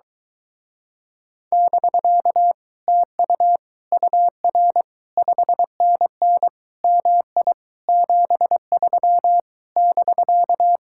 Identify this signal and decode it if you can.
Morse code, heard as a keyed tone, about 23 wpm, 700 Hz. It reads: <BK> TU UR 5NN MI 73 <BK>